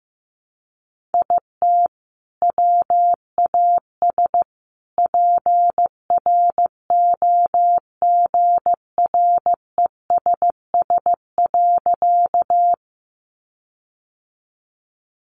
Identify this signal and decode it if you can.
Morse code, heard as a keyed tone, about 15 wpm, 700 Hz. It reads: IT WAS PROGRESS.
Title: Sentences from Top 1400 Words 15wpm